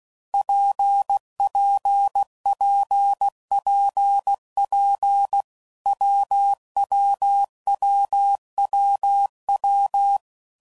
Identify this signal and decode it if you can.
Morse code, heard as a keyed tone, about 16 wpm, 800 Hz. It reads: PPPPP WWWWW
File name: Combinations / Difficult Combos / WPJ WPJ1